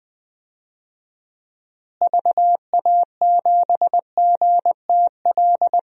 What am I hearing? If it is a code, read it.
Morse code, heard as a keyed tone, about 20 wpm, 700 Hz. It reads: VA7GTL
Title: DXCC-100 Call Signs - 6 Characters - Repeated 3x - 20wpm